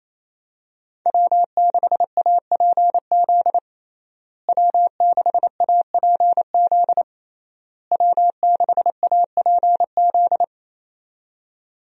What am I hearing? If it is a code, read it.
Morse code, heard as a keyed tone, about 28 wpm, 700 Hz. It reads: W6APZ W6APZ W6APZ
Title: USA Call Signs - 5 Characters - Repeated 3x - 28wpm